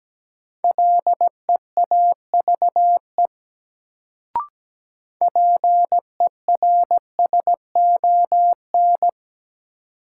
Morse code, heard as a keyed tone, about 17 wpm, 700 Hz. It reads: LEAVE PERSON
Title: Sentences from Top 400 Words - New Word Drill - 17wpm